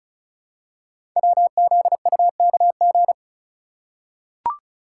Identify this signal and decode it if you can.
Morse code, heard as a keyed tone, about 36 wpm, 690 Hz. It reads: WZUKG
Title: Code Groups - 5 Letters - 35wpm